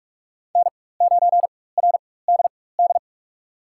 Morse code, heard as a keyed tone, about 46 wpm, spaced as for 21 wpm, 700 Hz. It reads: N9RDD
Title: USA Call Signs - 5 Characters - 4x Character Spacing - 45wpm